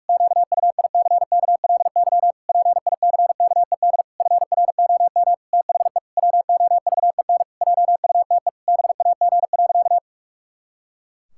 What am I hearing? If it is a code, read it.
Morse code, heard as a keyed tone, about 46 wpm, 700 Hz. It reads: QUICKLY PICKED FROM THE WOVEN JUTE BAG.
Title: Pangram Challenge v2 - 45wpm